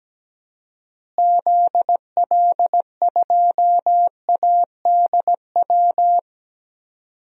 Morse code, heard as a keyed tone, about 17 wpm, 700 Hz. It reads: ZL2ADW
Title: DXCC-100 Call Signs - 6 Characters - Repeated 3x - 17wpm